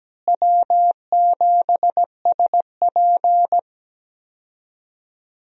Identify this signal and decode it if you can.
Morse code, heard as a keyed tone, about 17 wpm, 700 Hz. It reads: W7SP
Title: USA Call Signs - 4 Characters - Repeated 3x - 17wpm